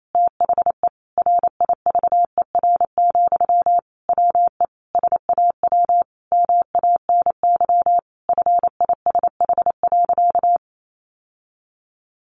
Morse code, heard as a keyed tone, about 28 wpm, 700 Hz. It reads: THE RIVER, WE SAW MANY FISH.
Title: Sentences from Top 1500 Words 28wpm